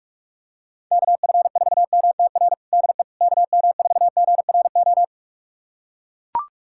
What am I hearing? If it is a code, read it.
Morse code, heard as a keyed tone, about 43 wpm, 700 Hz. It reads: KF4MTR DE KM4GRO
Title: USA Callsign DE Callsign 45wpm